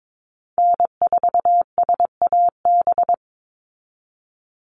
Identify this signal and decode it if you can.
Morse code, heard as a keyed tone, about 22 wpm, 700 Hz. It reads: N4SAB
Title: USA Call Signs 22wpm